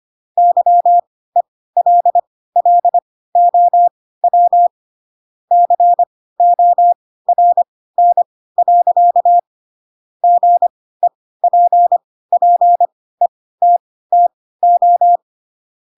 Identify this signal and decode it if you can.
Morse code, heard as a keyed tone, about 25 wpm, spaced as for 16 wpm, 700 Hz. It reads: YELLOW CORN. GEPPETTO
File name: Pinocchio / Pinocchio ch2 10 25 ik1hge